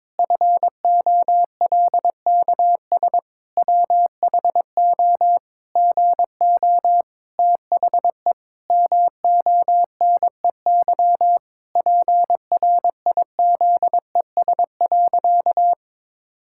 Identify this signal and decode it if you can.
Morse code, heard as a keyed tone, about 22 wpm, 700 Hz. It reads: FOLKS WHO GO THE MONEY PRIZES.
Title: Pangram Challenge v2 - 22wpm